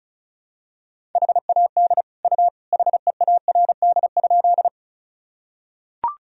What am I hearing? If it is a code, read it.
Morse code, heard as a keyed tone, about 36 wpm, 700 Hz. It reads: HAD U HEARD?
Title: Sentences from Top 400 Words and Abbreviations 35wpm